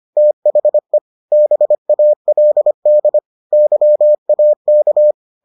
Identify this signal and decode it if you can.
Morse code, heard as a keyed tone, about 25 wpm, 600 Hz. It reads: THE BALD YAK